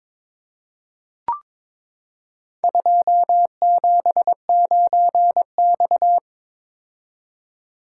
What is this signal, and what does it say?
Morse code, heard as a keyed tone, about 22 wpm, 700 Hz. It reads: 279X
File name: practice / 22wpm / Four Letter-Number Combinations - 22wpm